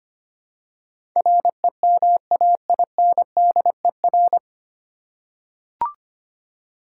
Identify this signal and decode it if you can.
Morse code, heard as a keyed tone, about 25 wpm, 690 Hz. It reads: REMAINDER
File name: practice / 25wpm / 9 Letter Words - First 3 Letters Given - 25wpm